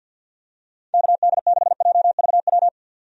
Morse code, heard as a keyed tone, about 51 wpm, 700 Hz. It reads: KD6JVW
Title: USA Call Signs - 6 Characters - 50wpm